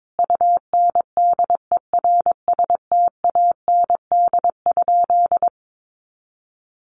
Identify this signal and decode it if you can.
Morse code, heard as a keyed tone, about 22 wpm, 700 Hz. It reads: UNDERSTAND?
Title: Sentences from Top 1100 Words 22wpm